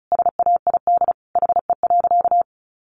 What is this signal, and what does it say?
Morse code, heard as a keyed tone, about 34 wpm, 700 Hz. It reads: SAID HE.
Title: Sentences from Top 100 Words 35wpm